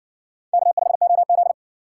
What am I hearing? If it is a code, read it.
Morse code, heard as a keyed tone, about 60 wpm, 700 Hz. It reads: K5OZ